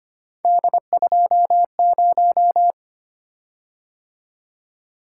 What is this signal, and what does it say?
Morse code, heard as a keyed tone, about 25 wpm, 700 Hz. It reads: D20
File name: Three Letter-Number Combinations - 25wpm